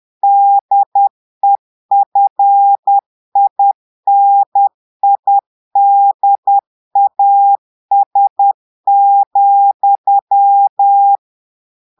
Morse code, heard as a keyed tone, about 10 wpm, 800 Hz. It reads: DEFINIDAS,
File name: portugais texte en clair 2 vitesse 10 mots minutes